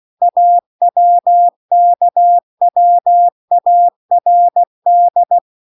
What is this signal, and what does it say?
Morse code, heard as a keyed tone, about 16 wpm, 700 Hz. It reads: AWKWARD
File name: CW-The-Big-Twoheaded-River-2-16WPM